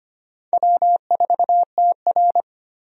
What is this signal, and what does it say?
Morse code, heard as a keyed tone, about 25 wpm, 700 Hz. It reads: W4TR
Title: USA Call Signs - 4 Characters - 25wpm